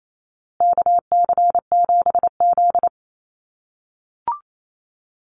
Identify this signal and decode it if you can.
Morse code, heard as a keyed tone, about 28 wpm, 700 Hz. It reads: KC7Z
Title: USA Call Signs Repeated 3x 28wpm